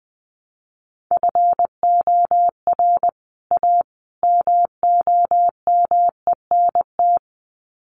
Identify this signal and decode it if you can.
Morse code, heard as a keyed tone, about 20 wpm, 700 Hz. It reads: FOR A MOMENT
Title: Sets of 3 Words 20wpm